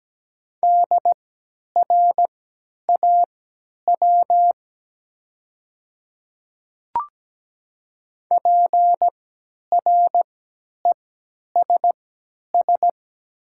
Morse code, heard as a keyed tone, about 17 wpm, spaced as for 10 wpm, 700 Hz. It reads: DRAW PRESS
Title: 3x Character Spacing - Top 300 Words - 17wpm